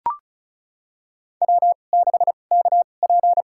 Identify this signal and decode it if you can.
Morse code, heard as a keyed tone, about 36 wpm, spaced as for 25 wpm, 700 Hz. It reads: W6KP
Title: USA Call Signs - 4 Characters - 2x Character Spacing - 35wpm